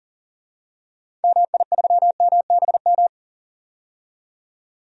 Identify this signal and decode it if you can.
Morse code, heard as a keyed tone, about 40 wpm, 700 Hz. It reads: MI3MBM
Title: DXCC-100 Call Signs - 6 Characters - 40wpm